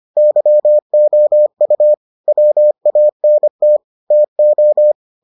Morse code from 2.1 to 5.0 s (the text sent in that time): WANT TO